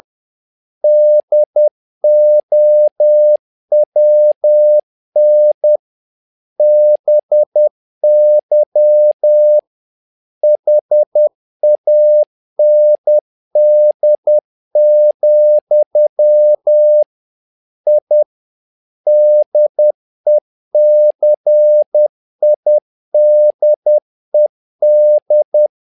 Morse code, heard as a keyed tone, about 10 wpm, 600 Hz. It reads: DOWN BY HAND, I DECIDED
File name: learning center / tower trouble / tower trouble 10wpm